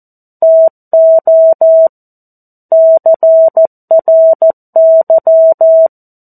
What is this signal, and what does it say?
Morse code, heard as a keyed tone, about 14 wpm, 650 Hz. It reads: TO CRY
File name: 02-Chapter-02 14 WPM